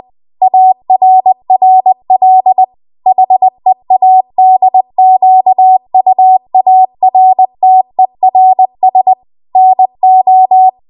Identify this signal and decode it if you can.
Morse code, heard as a keyed tone, about 20 wpm, 750 Hz. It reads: ARRL HEADQUARTERS NO
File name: Archive / 20 WPM / 20WPM